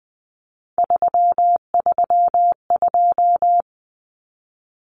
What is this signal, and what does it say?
Morse code, heard as a keyed tone, about 20 wpm, 700 Hz. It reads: 332
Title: Common Cut Numbers - 3-Digit 20wpm